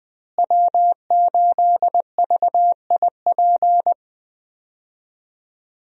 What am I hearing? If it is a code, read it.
Morse code, heard as a keyed tone, about 20 wpm, 700 Hz. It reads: W8VIP
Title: USA Call Signs - 5 Characters - Repeated 2x - 20wpm